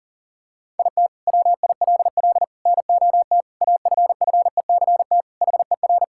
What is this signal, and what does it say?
Morse code, heard as a keyed tone, about 40 wpm, 700 Hz. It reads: IT WILL NOT AFFECT HER
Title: Sentences from Top 1000 Words - Review Part 6 - 40wpm